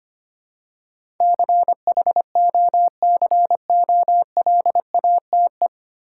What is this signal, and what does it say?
Morse code, heard as a keyed tone, about 25 wpm, 700 Hz. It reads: CHOCOLATE
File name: Nine Letter Words - 25wpm